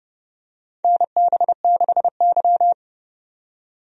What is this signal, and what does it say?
Morse code, heard as a keyed tone, about 30 wpm, 700 Hz. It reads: NB6Y
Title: USA Call Signs - 4 Characters - 30wpm